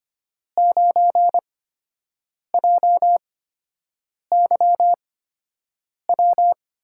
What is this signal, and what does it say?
Morse code, heard as a keyed tone, about 25 wpm, spaced as for 7 wpm, 700 Hz. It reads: 9JYW
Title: USA Call Signs - 8x Character Spacing - 25wpm